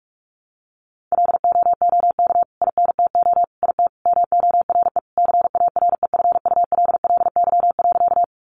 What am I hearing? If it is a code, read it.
Morse code, heard as a keyed tone, about 44 wpm, 700 Hz. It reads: LOOK INTO IT MORE CAREFULLY.
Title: Sentences from Top 1300 Words 45wpm